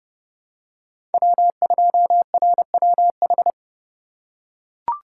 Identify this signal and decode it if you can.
Morse code, heard as a keyed tone, about 30 wpm, 700 Hz. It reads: W2RWH
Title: USA Call Signs - 5 Characters - 3x Character Spacing - 30wpm